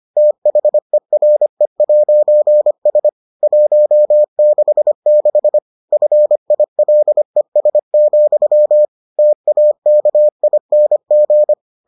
Morse code, heard as a keyed tone, about 25 wpm, 600 Hz. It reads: THERE'S 166 FILES, TAKING